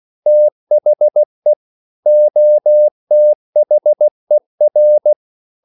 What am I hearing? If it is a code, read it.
Morse code, heard as a keyed tone, about 16 wpm, 600 Hz. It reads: THE OTHER
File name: CW/Practice